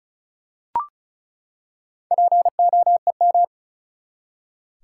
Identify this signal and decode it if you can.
Morse code, heard as a keyed tone, about 35 wpm, 700 Hz. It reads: POEM
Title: Four Letter Words - 35wpm